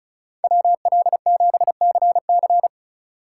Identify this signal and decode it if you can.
Morse code, heard as a keyed tone, about 36 wpm, 700 Hz. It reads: WL7CC
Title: USA Call Signs - 5 Characters - 2x Character Spacing - 35wpm